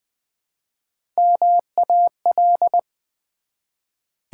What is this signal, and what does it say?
Morse code, heard as a keyed tone, about 20 wpm, 700 Hz. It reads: MAL
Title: Build A Word - 1 Letter At A Time - 4-Letter Words - 20wpm